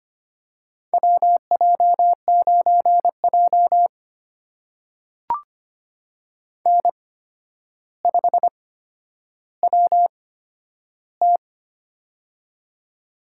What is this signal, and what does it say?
Morse code, heard as a keyed tone, about 25 wpm, spaced as for 11 wpm, 700 Hz. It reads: WJ9J N5WT
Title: USA Call Signs - 4 Characters - 8x Character Spacing - 25wpm